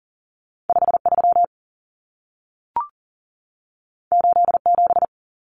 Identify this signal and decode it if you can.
Morse code, heard as a keyed tone, about 40 wpm, 700 Hz. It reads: 5 3 8 7
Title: Common Cut Numbers - 2-Digit 40wpm